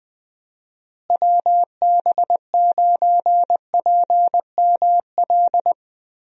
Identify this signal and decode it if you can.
Morse code, heard as a keyed tone, about 19 wpm, 700 Hz. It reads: WB9PML
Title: USA Call Signs - 6 Characters - Repeated 2x - 20wpm